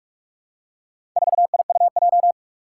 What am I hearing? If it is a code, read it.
Morse code, heard as a keyed tone, about 43 wpm, 700 Hz. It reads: 4IUJ
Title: Four Challenging Letter-Number Combinations 45wpm